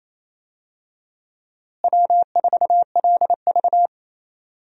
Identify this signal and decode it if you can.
Morse code, heard as a keyed tone, about 27 wpm, 700 Hz. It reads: W4LV
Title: USA Call Signs - 4 Characters - Repeated 2x - 28wpm